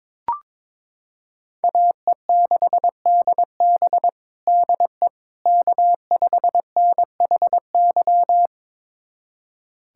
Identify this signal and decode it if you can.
Morse code, heard as a keyed tone, about 22 wpm, 700 Hz. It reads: AE6DB DE K5NHY
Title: USA Callsign DE Callsign - 5 Characters - 22wpm